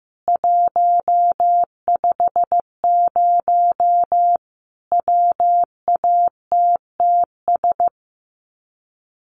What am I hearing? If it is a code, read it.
Morse code, heard as a keyed tone, about 15 wpm, 700 Hz. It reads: 150 WATTS
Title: Common QSO Phrases 15wpm